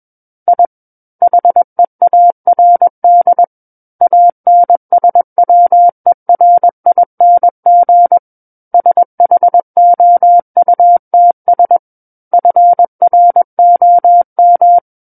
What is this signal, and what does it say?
Morse code, heard as a keyed tone, about 21 wpm, 700 Hz. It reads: I HEARD ANSWERING SHOUTS FROM
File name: Book One / 12-Chapter 12 - 21 WPM